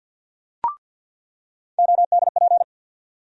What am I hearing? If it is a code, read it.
Morse code, heard as a keyed tone, about 47 wpm, 700 Hz. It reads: ODP
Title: USA Call Sign Postfixes - 3 Letters 50wpm